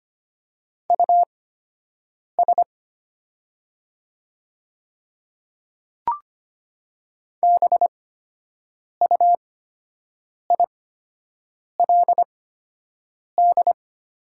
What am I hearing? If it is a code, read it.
Morse code, heard as a keyed tone, about 25 wpm, spaced as for 7 wpm, 700 Hz. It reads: US BUILD
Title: 8x Character Spacing - Top 200 Words - 25wpm